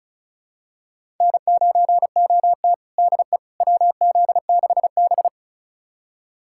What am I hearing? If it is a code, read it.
Morse code, heard as a keyed tone, about 36 wpm, 690 Hz. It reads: N9OT DE WZ6B
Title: USA Callsign DE Callsign - 4 Characters - 35wpm